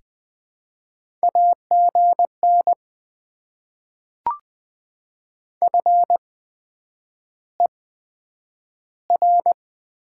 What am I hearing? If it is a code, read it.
Morse code, heard as a keyed tone, about 21 wpm, spaced as for 9 wpm, 700 Hz. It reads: AGN FER
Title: Top 20 QSO Elements - 8x Character Spacing - 20wpm